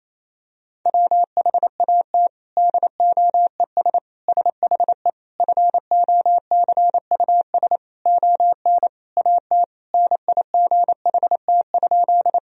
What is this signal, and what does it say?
Morse code, heard as a keyed tone, about 28 wpm, 700 Hz. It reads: WHAT DOES SHE FOCUS ON AT NIGHT?
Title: Sentences from Top 1000 Words - Review Part 4 - 28wpm